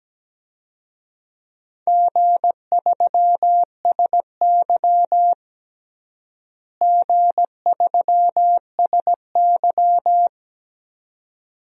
Morse code, heard as a keyed tone, about 17 wpm, 700 Hz. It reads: G3SY G3SY
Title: DXCC-100 Call Signs - 4 Characters - Repeated 2x - 17wpm